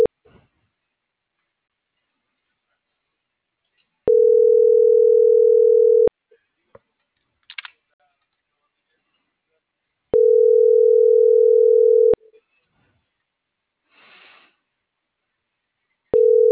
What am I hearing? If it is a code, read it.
Ringback tone (North America), 440 + 480 Hz.